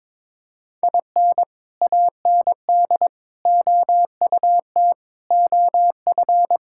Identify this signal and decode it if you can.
Morse code, heard as a keyed tone, about 22 wpm, 700 Hz. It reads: IN AND OUT OF
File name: practice / 22wpm / Sentences with 4-Word Phrases - 22wpm